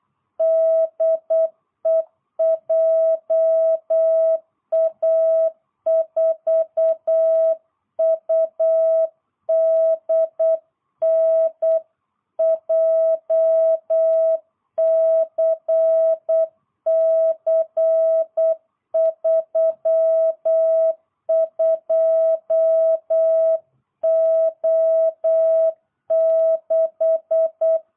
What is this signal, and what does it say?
Morse code, heard as a keyed tone, about 8 wpm, 660 Hz. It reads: DE JA4UDN JCC32O6
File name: jh8cbh / 2022cw